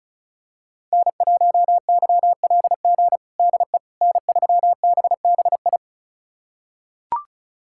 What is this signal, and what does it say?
Morse code, heard as a keyed tone, about 34 wpm, 700 Hz. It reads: N1YLG DE N3BBI
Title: USA Callsign DE Callsign - 5 Characters - 35wpm